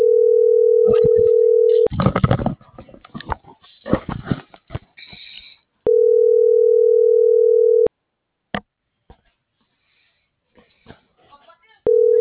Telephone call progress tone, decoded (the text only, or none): ringback tone